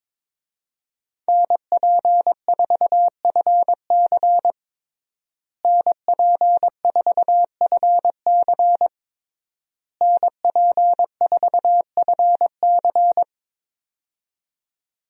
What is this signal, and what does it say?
Morse code, heard as a keyed tone, about 22 wpm, 700 Hz. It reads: NP4FC NP4FC NP4FC
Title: USA Call Signs - 5 Characters - Repeated 3x - 22wpm